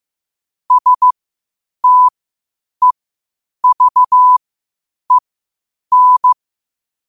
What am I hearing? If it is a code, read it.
Morse code, heard as a keyed tone, about 15 wpm, spaced as for 8 wpm, 1000 Hz. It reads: STEVEN
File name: qotdfiles / 8wpm